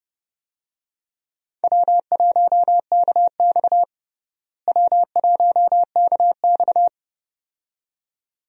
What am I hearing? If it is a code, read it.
Morse code, heard as a keyed tone, about 30 wpm, 700 Hz. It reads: W1KX W1KX
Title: USA Call Signs - 4 Characters - Repeated 2x - 30wpm